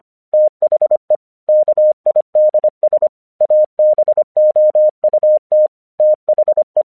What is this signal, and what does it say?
Morse code, heard as a keyed tone, about 25 wpm, 610 Hz. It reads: THE KIDS ABOUT THE
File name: 1 English 25wpm / AQC14